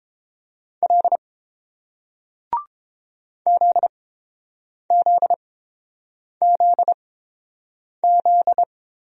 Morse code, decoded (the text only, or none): L ZZZZ